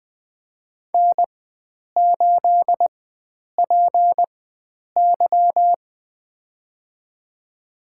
Morse code, heard as a keyed tone, about 20 wpm, spaced as for 9 wpm, 700 Hz. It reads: N8PY
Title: USA Call Signs - 4 Characters - 4x Character Spacing - 20wpm